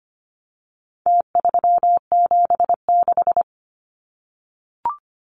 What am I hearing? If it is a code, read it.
Morse code, heard as a keyed tone, about 25 wpm, 700 Hz. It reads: T376
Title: Numbers - Contest Style - 4-Digit 25wpm